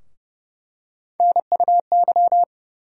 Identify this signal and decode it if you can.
Morse code, heard as a keyed tone, about 30 wpm, 700 Hz. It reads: NUY
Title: USA Call Sign Postfixes - 3 Letters 30wpm